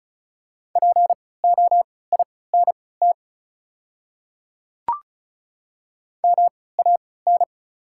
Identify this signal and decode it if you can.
Morse code, heard as a keyed tone, about 36 wpm, spaced as for 20 wpm, 700 Hz. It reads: POINT MAN
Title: 3x Character Spacing - Top 200 Words - 35wpm